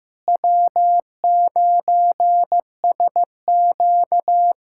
Morse code, heard as a keyed tone, about 15 wpm, 700 Hz. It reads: W9SQ
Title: USA Callsign DE Callsign - 4 Characters - 15wpm